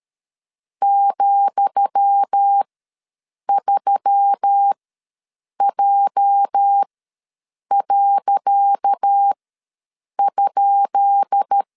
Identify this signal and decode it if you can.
Morse code, heard as a keyed tone, about 13 wpm, spaced as for 7 wpm, 800 Hz. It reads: ,3J.?
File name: LEZIONI / 10 SIGNS / 10D COMA ws 12wpm 3618